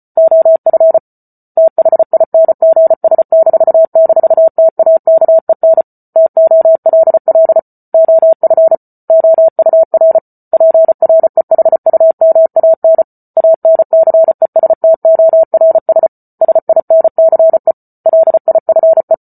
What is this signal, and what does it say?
Morse code, heard as a keyed tone, about 34 wpm, 650 Hz. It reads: OF THINGS--TAKEN TOLL OF OUR PREHUMAN ANCESTORS SINCE LIFE
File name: Book Two / 08-Chapter 8 - 34 WPM